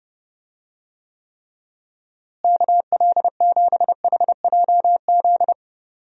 Morse code, heard as a keyed tone, about 30 wpm, 700 Hz. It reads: KL7HJZ
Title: DXCC-100 Call Signs - 6 Characters - Repeated 2x - 30wpm